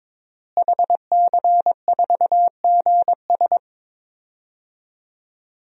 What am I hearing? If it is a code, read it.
Morse code, heard as a keyed tone, about 22 wpm, 700 Hz. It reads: HC4GS
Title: DXCC-100 Call Signs - 5 Characters - Repeated 3x - 22wpm